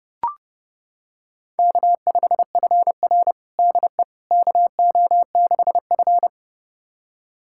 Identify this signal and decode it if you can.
Morse code, heard as a keyed tone, about 30 wpm, 700 Hz. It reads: K5FR DE KO6F
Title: USA Callsign DE Callsign - 4 Characters - 30wpm